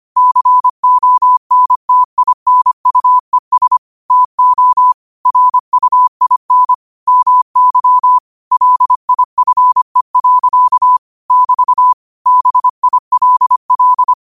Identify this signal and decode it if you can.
Morse code, heard as a keyed tone, about 25 wpm, 1000 Hz. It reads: CONTINUES TO RUIN MY LIFE. <BT> BILL